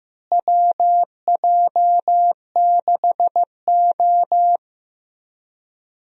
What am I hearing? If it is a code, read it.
Morse code, heard as a keyed tone, about 15 wpm, 700 Hz. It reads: WJ6O
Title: USA Call Signs - 4 Characters - Repeated 2x - 15wpm